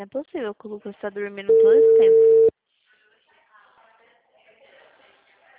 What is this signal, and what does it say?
Ringback tone (China, 450 Hz), with speech.